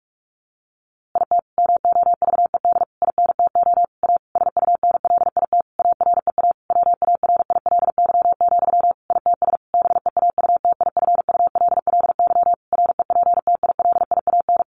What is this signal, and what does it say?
Morse code, heard as a keyed tone, about 44 wpm, 700 Hz. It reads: IT MOVED INTO A SUNLIT AREA WARILY, ITS BEAUTIFULLY REPTILIAN
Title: A Sound Of Thunder 45wpm